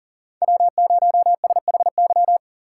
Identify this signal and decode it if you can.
Morse code, heard as a keyed tone, about 52 wpm, 700 Hz. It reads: W0SHY